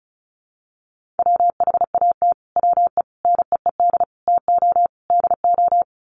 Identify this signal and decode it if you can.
Morse code, heard as a keyed tone, about 34 wpm, 700 Hz. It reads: WHAT WE NEED TO DO
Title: Sets of 5 Words Encore 35wpm